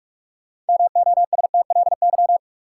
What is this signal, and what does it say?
Morse code, heard as a keyed tone, about 46 wpm, 700 Hz. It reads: MOSTLY
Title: Sentences from Top 1300 Words - New Word Drill Alt - 45wpm